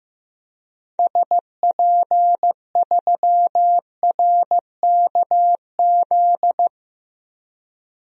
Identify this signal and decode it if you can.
Morse code, heard as a keyed tone, about 15 wpm, 700 Hz. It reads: SP3RKZ
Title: DXCC-100 Call Signs - 6 Characters - 15wpm